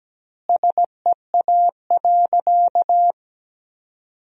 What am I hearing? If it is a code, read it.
Morse code, heard as a keyed tone, about 17 wpm, 700 Hz. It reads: SEA.
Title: 2x Word-Spacing - Sentences from Top 400 Words - 17wpm